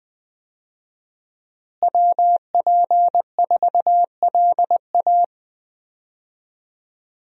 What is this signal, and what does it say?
Morse code, heard as a keyed tone, about 20 wpm, 700 Hz. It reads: WP4LA
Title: USA Call Signs - 5 Characters - Repeated 3x - 20wpm